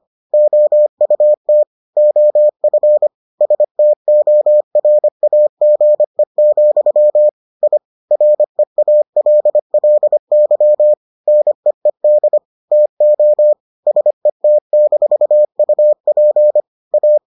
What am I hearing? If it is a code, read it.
Morse code, heard as a keyed tone, about 25 wpm, 600 Hz. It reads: OUT OF STORAGE, I REALLY NEED TO SET-UP A